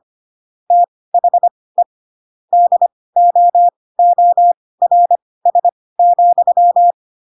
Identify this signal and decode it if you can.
Morse code, heard as a keyed tone, about 25 wpm, spaced as for 18 wpm, 700 Hz. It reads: THE DOORS,
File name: Pinocchio / Pinocchio ch6 12 25 ik1hge